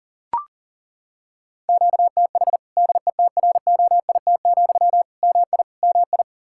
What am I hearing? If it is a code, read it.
Morse code, heard as a keyed tone, about 40 wpm, 700 Hz. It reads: QTH DETROIT, MI MI
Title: Practice QSO - Exchange 1 - Beginning - 40wpm